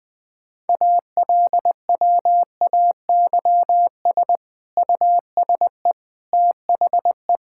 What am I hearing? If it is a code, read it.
Morse code, heard as a keyed tone, about 20 wpm, 700 Hz. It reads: ALWAYS USE THE
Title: Sentences from Top 1000 Words - Review Part 6 - 20wpm